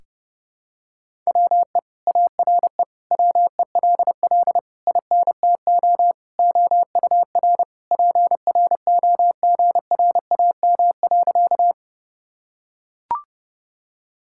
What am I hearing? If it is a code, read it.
Morse code, heard as a keyed tone, about 30 wpm, 700 Hz. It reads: WE ARE WELL INTO OUR PROGRAM.
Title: Sentences from Top 1000 Words - Review Part 5 - 30wpm